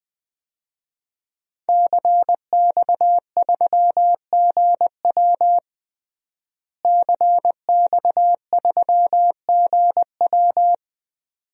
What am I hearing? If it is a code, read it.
Morse code, heard as a keyed tone, about 20 wpm, 700 Hz. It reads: CX3GW CX3GW
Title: DXCC-100 Call Signs - 5 Characters - Repeated 2x - 20wpm